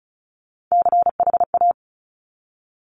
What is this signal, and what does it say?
Morse code, heard as a keyed tone, about 36 wpm, 700 Hz. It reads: CHA
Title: Top Three Letter Combinations in Words 35wpm